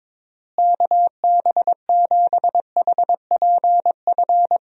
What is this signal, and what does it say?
Morse code, heard as a keyed tone, about 21 wpm, 700 Hz. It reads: KB7HPF
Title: USA Call Signs - 6 Characters - Repeated 2x - 22wpm